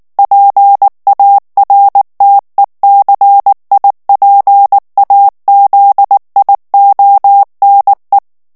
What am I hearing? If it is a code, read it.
Morse code, heard as a keyed tone, about 19 wpm, 800 Hz. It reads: PARTECIPAZIONE
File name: morse mp3 / ITALIEN texte clair 2 vitesse 18 mots minute